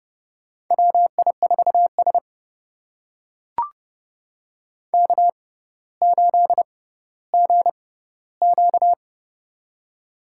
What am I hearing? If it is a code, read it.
Morse code, heard as a keyed tone, about 32 wpm, spaced as for 16 wpm, 700 Hz. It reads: WI4S K8GQ
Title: USA Call Signs - 4 Characters - 6x Character Spacing - 30wpm